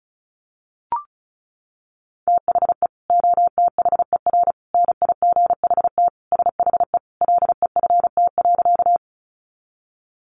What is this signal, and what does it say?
Morse code, heard as a keyed tone, about 34 wpm, 700 Hz. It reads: THE OTHER NIGHT SHE LEFT.
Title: Sentences from Top 300 Words 35wpm